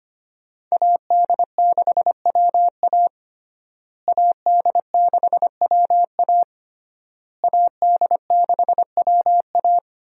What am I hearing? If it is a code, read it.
Morse code, heard as a keyed tone, about 25 wpm, 700 Hz. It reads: AD6WA AD6WA AD6WA
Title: USA Call Signs - 5 Characters - Repeated 3x - 25wpm